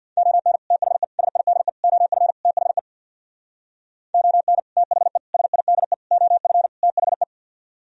Morse code, heard as a keyed tone, about 64 wpm, 690 Hz. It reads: ON THE SIDE OF THE ON THE SIDE OF THE